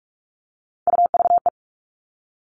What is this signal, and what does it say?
Morse code, heard as a keyed tone, about 44 wpm, 700 Hz. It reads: UVE
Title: All Cut Numbers - 3-Digit 45wpm